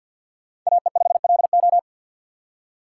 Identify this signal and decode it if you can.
Morse code, heard as a keyed tone, about 47 wpm, 700 Hz. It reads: AE5LO
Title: USA Call Signs - 6x Character Spacing - 50wpm